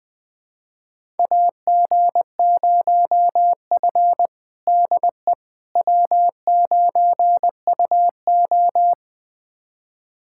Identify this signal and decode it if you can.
Morse code, heard as a keyed tone, about 21 wpm, 700 Hz. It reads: AG0F DE W9UO
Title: USA Callsign DE Callsign - 4 Characters - 20wpm